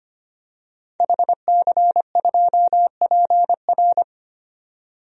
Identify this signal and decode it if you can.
Morse code, heard as a keyed tone, about 25 wpm, 700 Hz. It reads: HC2PR
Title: DXCC-100 Call Signs - 5 Characters - 25wpm